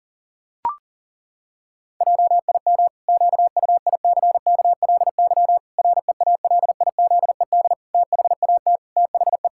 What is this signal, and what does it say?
Morse code, heard as a keyed tone, about 40 wpm, 700 Hz. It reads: JIM QUICKLY REALIZED THAT THE
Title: Pangram Challenge v2 - 40wpm